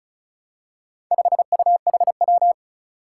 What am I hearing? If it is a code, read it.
Morse code, heard as a keyed tone, about 36 wpm, 700 Hz. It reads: 5UHW
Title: Four Challenging Letter-Number Combinations 35wpm